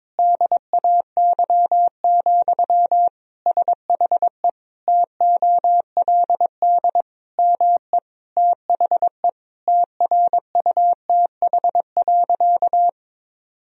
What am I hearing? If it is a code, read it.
Morse code, heard as a keyed tone, about 22 wpm, 700 Hz. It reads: DAY, SHE TOLD ME THE TRUTH.
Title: Sentences from Top 1000 Words - Review Part 4 - 22wpm